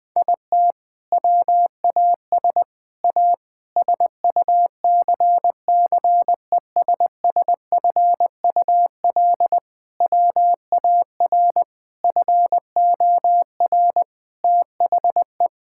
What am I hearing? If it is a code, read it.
Morse code, heard as a keyed tone, about 21 wpm, 700 Hz. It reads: IT WAS A SUCCESSFUL WAR FOR THE
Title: Sentences from Top 1000 Words - Review Part 6 - 20wpm